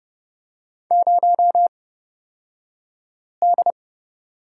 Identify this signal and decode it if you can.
Morse code, heard as a keyed tone, about 29 wpm, spaced as for 5 wpm, 700 Hz. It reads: 0D
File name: Lesson 36d - BT - Two Characters 30wpm